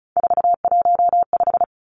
Morse code, heard as a keyed tone, about 34 wpm, 700 Hz. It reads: 415
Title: Numbers - 3-Digit 35wpm